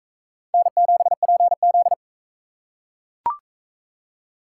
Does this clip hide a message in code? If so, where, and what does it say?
Yes: Morse code from 0.5 to 1.9 s, heard as a keyed tone, about 38 wpm, 700 Hz. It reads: N7PZ